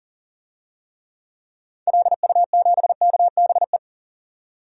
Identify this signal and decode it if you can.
Morse code, heard as a keyed tone, about 40 wpm, 700 Hz. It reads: LU7KBE
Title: DXCC-100 Call Signs - 6 Characters - 40wpm